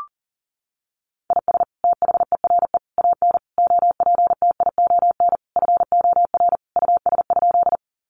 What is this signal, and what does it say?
Morse code, heard as a keyed tone, about 40 wpm, 700 Hz. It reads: IS THERE AN OPTION FOR US?
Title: Sentences from Top 1300 Words 40wpm